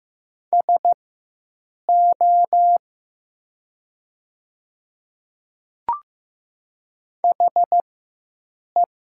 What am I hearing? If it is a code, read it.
Morse code, heard as a keyed tone, about 15 wpm, spaced as for 7 wpm, 700 Hz. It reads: SO HE
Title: 4x Character Spacing - Top 100 Words - 15wpm